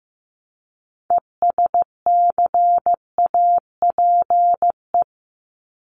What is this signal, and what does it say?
Morse code, heard as a keyed tone, about 15 wpm, 700 Hz. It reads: ESCAPE
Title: Top 1501-1750 Words 15wpm